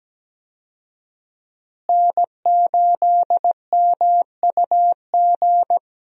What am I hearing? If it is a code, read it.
Morse code, heard as a keyed tone, about 17 wpm, 700 Hz. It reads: N8MUG
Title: USA Call Signs - 5 Characters - Repeated 3x - 17wpm